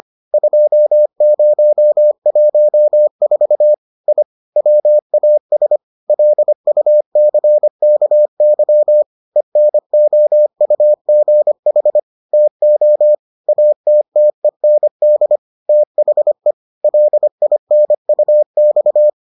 Morse code, heard as a keyed tone, about 25 wpm, 600 Hz. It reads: 2014 I WAS LUCKY ENOUGH TO ATTEND THE LINUX